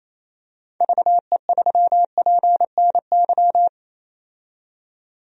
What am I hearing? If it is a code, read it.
Morse code, heard as a keyed tone, about 28 wpm, 700 Hz. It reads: VE3PNY
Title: DXCC-100 Call Signs - 6 Characters - 28wpm